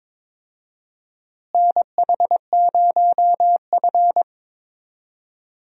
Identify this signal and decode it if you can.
Morse code, heard as a keyed tone, about 22 wpm, 700 Hz. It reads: NH0F
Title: USA Call Signs - 4 Characters - Repeated 2x - 22wpm